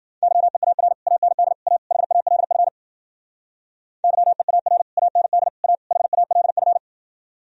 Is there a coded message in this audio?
Yes: Morse code from 0.2 to 6.8 s, heard as a keyed tone, about 52 wpm, 700 Hz. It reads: YEAR AND A HALF YEAR AND A HALF